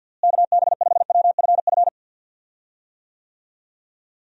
Morse code, heard as a keyed tone, about 52 wpm, 700 Hz. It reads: KB5WFF
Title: USA Call Signs - 6 Characters - Repeated 3x - 50wpm